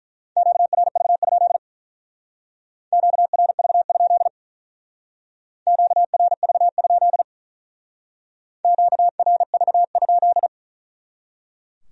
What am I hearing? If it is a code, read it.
Morse code, heard as a keyed tone, about 41 wpm, 700 Hz. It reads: QRV? QRV? QRV? QRV?
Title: Speed-Racing Q-Codes - 35wpm